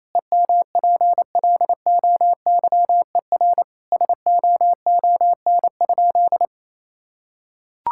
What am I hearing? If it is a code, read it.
Morse code, heard as a keyed tone, about 28 wpm, 700 Hz. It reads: EMPLOYER SOON?
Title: Sentences from Top 1600 Words 28wpm